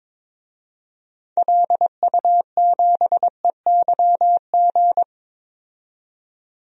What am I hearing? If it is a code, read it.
Morse code, heard as a keyed tone, about 22 wpm, 700 Hz. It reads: LU7EYG
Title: DXCC-100 Call Signs - 6 Characters - 22wpm